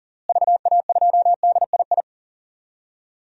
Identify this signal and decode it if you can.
Morse code, heard as a keyed tone, about 38 wpm, 700 Hz. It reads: VA2DII
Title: DXCC-100 Call Signs - 6 Characters - Repeated 2x - 40wpm